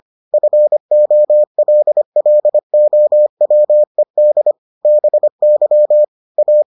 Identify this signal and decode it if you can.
Morse code, heard as a keyed tone, about 25 wpm, 600 Hz. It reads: FOLLOWED BY A